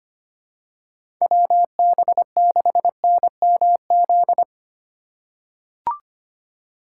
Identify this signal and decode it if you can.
Morse code, heard as a keyed tone, about 25 wpm, 700 Hz. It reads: WB6NMZ
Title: USA Call Signs - 6 Characters - 4x Character Spacing - 25wpm